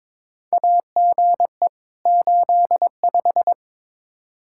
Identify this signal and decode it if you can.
Morse code, heard as a keyed tone, about 22 wpm, 700 Hz. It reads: AGE 85
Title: Practice QSO - Exchange 3 - More Info - 22wpm